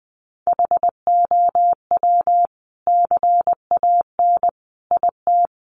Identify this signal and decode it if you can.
Morse code, heard as a keyed tone, about 20 wpm, 700 Hz. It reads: HOW CAN IT
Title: Sentences from Top 900 Words 20wpm